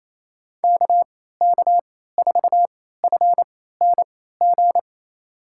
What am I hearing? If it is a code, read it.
Morse code, heard as a keyed tone, about 27 wpm, spaced as for 16 wpm, 700 Hz. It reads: KK4FNG
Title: USA Call Signs - 6 Characters - 3x Character Spacing - 28wpm